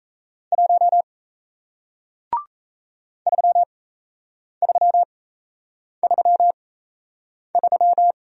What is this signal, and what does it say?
Morse code, heard as a keyed tone, about 37 wpm, spaced as for 8 wpm, 700 Hz. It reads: 1 3333